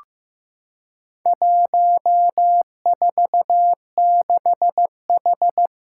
Morse code, heard as a keyed tone, about 15 wpm, 700 Hz. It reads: 146H
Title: Four Challenging Letter-Number Combinations 15wpm